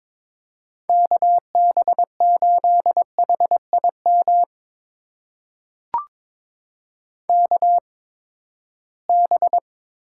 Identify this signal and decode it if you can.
Morse code, heard as a keyed tone, about 22 wpm, spaced as for 15 wpm, 700 Hz. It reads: KB8HIM KB
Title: USA Call Signs - 6 Characters - 8x Character Spacing - 22wpm